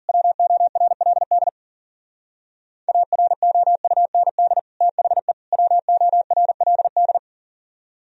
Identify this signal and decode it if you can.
Morse code, heard as a keyed tone, about 50 wpm, 700 Hz. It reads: WORLD AROUND THE WORLD
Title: Speed-Racing - Sets of 3 Words 40wpm